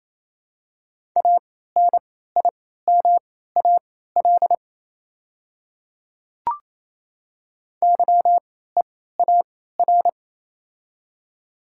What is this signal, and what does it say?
Morse code, heard as a keyed tone, about 28 wpm, spaced as for 16 wpm, 700 Hz. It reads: ANIMAL YEAR
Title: 3x Character Spacing - Top 300 Words - 28wpm